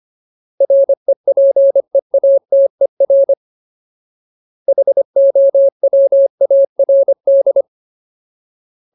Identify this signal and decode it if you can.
Morse code, heard as a keyed tone, about 25 wpm, 550 Hz. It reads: REPEATER HOWARD
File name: Ultimate Word List / 25WPM Ultimate